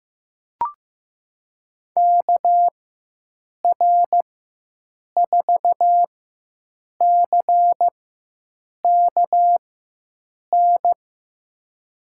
Morse code, heard as a keyed tone, about 15 wpm, spaced as for 7 wpm, 700 Hz. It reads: KR4CKN
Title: USA Call Signs - 6 Characters - 4x Character Spacing - 15wpm